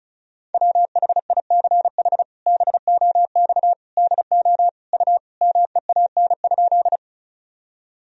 Morse code, heard as a keyed tone, about 36 wpm, 700 Hz. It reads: WHICH BOX DO U MEAN?
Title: Sentences from Top 500 Words and Abbreviations 35wpm